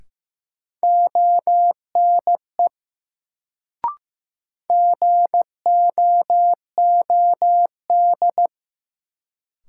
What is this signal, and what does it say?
Morse code, heard as a keyed tone, about 15 wpm, 700 Hz. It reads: ONE GOOD
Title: Sets of 2 Words - New Word Drill - 15wpm